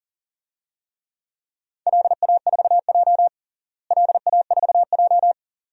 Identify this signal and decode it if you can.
Morse code, heard as a keyed tone, about 40 wpm, 700 Hz. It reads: LA4J LA4J
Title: DXCC-100 Call Signs - 4 Characters - Repeated 2x - 40wpm